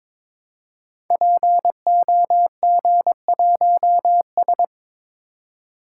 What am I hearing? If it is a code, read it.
Morse code, heard as a keyed tone, about 22 wpm, 700 Hz. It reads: POG1S
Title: Code Groups - 5 Letter:Numbers - 22wpm